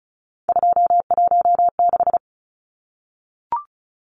Morse code, heard as a keyed tone, about 34 wpm, 700 Hz. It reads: 216
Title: Numbers - 3-Digit 35wpm